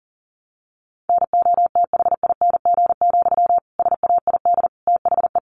Morse code, heard as a keyed tone, about 40 wpm, 700 Hz. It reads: NOTHING, SAID THE
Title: Sentences from Top 1300 Words 40wpm